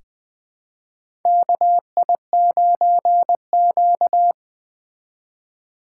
Morse code, heard as a keyed tone, about 20 wpm, 700 Hz. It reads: KI9Q
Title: USA Call Signs - 4 Characters - 2x Character Spacing - 20wpm